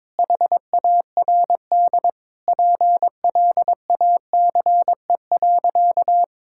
Morse code, heard as a keyed tone, about 22 wpm, 700 Hz. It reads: HARD PLACE.
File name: practice / 22wpm / Sentences with 8-Word Phrases - 22wpm